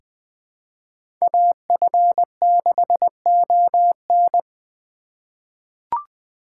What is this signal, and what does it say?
Morse code, heard as a keyed tone, about 20 wpm, 700 Hz. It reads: AF6ON
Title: USA Call Signs - 8x Character Spacing - 20wpm